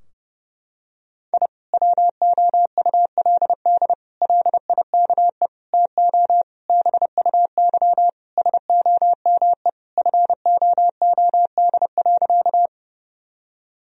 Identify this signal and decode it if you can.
Morse code, heard as a keyed tone, about 30 wpm, 700 Hz. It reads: I WOULD LIKE TO BUY SOME FOOD.
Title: Sentences from Top 1000 Words - Review Part 4 - 30wpm